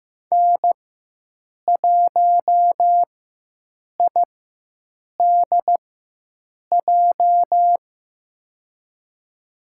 Morse code, heard as a keyed tone, about 15 wpm, spaced as for 7 wpm, 700 Hz. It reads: N1IDJ
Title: USA Call Signs - 5 Characters - 4x Character Spacing - 15wpm